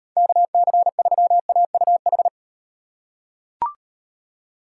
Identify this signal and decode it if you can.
Morse code, heard as a keyed tone, about 38 wpm, 700 Hz. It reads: KC3AUH